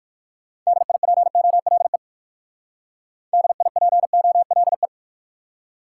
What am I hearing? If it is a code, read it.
Morse code, heard as a keyed tone, about 57 wpm, 700 Hz. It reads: DIPOLE DIPOLE